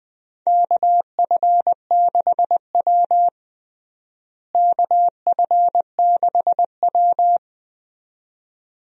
Morse code, heard as a keyed tone, about 21 wpm, 700 Hz. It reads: KF6W KF6W
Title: USA Call Signs - 4 Characters - Repeated 2x - 20wpm